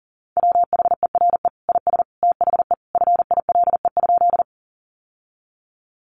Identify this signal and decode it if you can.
Morse code, heard as a keyed tone, about 40 wpm, 700 Hz. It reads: WHERE IS THE FILE?
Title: Sentences from Top 1200 Words 40wpm